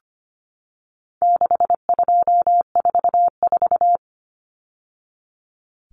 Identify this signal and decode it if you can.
Morse code, heard as a keyed tone, about 25 wpm, 700 Hz. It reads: 6244
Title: Numbers - 4-Digit 25wpm